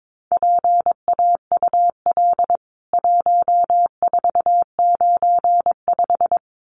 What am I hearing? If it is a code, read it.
Morse code, heard as a keyed tone, about 22 wpm, 700 Hz. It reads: PAUL 1495
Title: CWT Exchange 22wpm